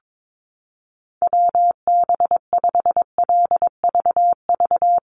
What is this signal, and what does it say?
Morse code, heard as a keyed tone, about 22 wpm, 700 Hz. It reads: WB5LVV
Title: USA Call Signs Repeated 2x 22wpm